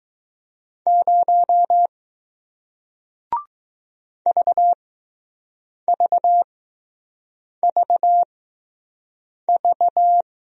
Morse code, heard as a keyed tone, about 20 wpm, spaced as for 6 wpm, 700 Hz. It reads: 0 VVVV